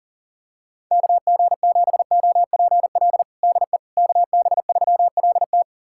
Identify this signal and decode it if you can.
Morse code, heard as a keyed tone, about 40 wpm, 700 Hz. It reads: KG7OPL DE KB3LT
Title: USA Callsign DE Callsign 40wpm